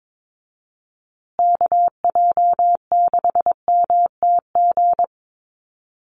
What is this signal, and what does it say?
Morse code, heard as a keyed tone, about 22 wpm, 700 Hz. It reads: KJ6MTG
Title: USA Call Signs Repeated 3x 22wpm